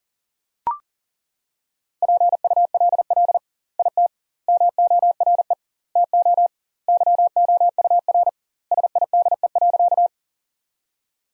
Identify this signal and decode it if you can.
Morse code, heard as a keyed tone, about 40 wpm, 700 Hz. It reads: PULL IT MORE TO YOUR SIDE.
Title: 2x Word-Spacing - Sentences from Top 400 Words - 40wpm